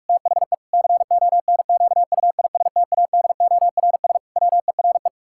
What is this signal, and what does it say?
Morse code, heard as a keyed tone, about 48 wpm, 700 Hz. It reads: THE CONQUISTADORS WERE
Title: Pangram Challenge v2 - 45wpm